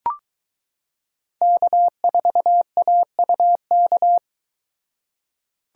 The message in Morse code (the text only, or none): K4AUK